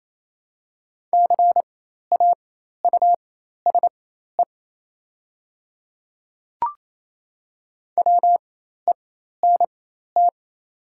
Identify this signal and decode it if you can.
Morse code, heard as a keyed tone, about 28 wpm, spaced as for 13 wpm, 690 Hz. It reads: CAUSE WENT
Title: 4x Character Spacing - Top 300 Words - 28wpm